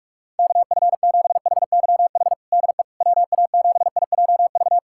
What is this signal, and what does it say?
Morse code, heard as a keyed tone, about 42 wpm, 700 Hz. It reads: KF7HYH DE WA7IJV
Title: USA Callsign DE Callsign - 6 Characters - 45wpm